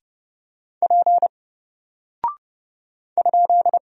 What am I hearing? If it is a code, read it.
Morse code, heard as a keyed tone, about 30 wpm, 700 Hz. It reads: P ?
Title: Single Character (Letter:Number:Punctuation:Prosign) Alt 30wpm